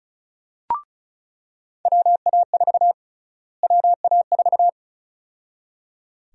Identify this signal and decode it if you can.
Morse code, heard as a keyed tone, about 34 wpm, 700 Hz. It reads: WA4 WA4
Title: USA Call Sign Prefixes - Repeated 2x - 35wpm